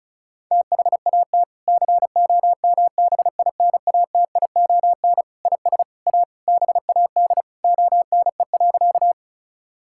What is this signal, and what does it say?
Morse code, heard as a keyed tone, about 34 wpm, 700 Hz. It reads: THAT COMBINATION IS A BAD ONE.
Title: Sentences from Top 1000 Words - Review Part 6 - 35wpm